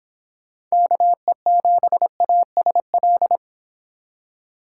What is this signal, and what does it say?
Morse code, heard as a keyed tone, about 26 wpm, 700 Hz. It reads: KE7ASL
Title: Speed Racing - USA 2x3 Call Signs 22wpm